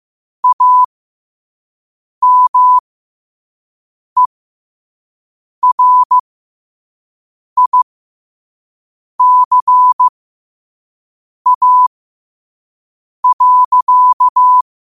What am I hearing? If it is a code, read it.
Morse code, heard as a keyed tone, about 15 wpm, spaced as for 5 wpm, 1000 Hz. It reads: AMERICA.
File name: qotdfiles / 5wpm